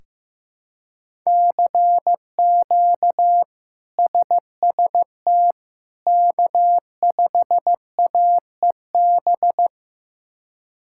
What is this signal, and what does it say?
Morse code, heard as a keyed tone, about 15 wpm, 700 Hz. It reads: CQ SST K5AEB
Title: K1USN SST QSO Practice - 15wpm